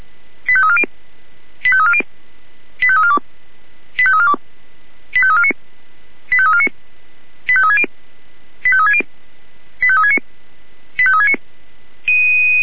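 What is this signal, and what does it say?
Five-tone selective call (ZVEI), heard as quick runs of five tones, tones of about 70 ms. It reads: 85340 85340 85342 85342 85349 85349 85360 85360 85369 85369
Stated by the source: ZVEI five-tone selective call received on an SDR.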